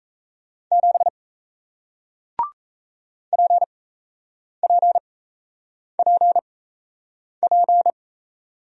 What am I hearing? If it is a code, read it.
Morse code, heard as a keyed tone, about 37 wpm, spaced as for 8 wpm, 700 Hz. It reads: 7 PPPP